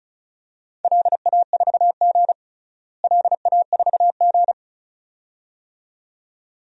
Morse code, heard as a keyed tone, about 36 wpm, 700 Hz. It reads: LA4G LA4G
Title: DXCC-100 Call Signs - 4 Characters - Repeated 2x - 35wpm